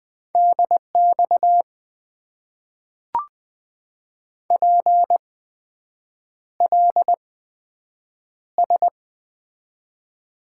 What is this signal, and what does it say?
Morse code, heard as a keyed tone, about 20 wpm, spaced as for 7 wpm, 700 Hz. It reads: DX PLS
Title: Top 90 QSO Elements - 8x Character Spacing - 20wpm